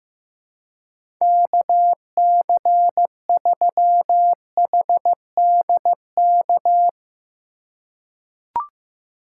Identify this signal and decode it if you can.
Morse code, heard as a keyed tone, about 15 wpm, 700 Hz. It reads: KC3HDK
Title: USA Call Signs - 6 Characters - 3x Character Spacing - 15wpm